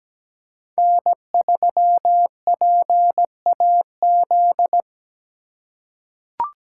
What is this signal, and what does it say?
Morse code, heard as a keyed tone, about 17 wpm, 700 Hz. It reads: N3PAZ
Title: USA Call Signs - 4x Character Spacing - 17wpm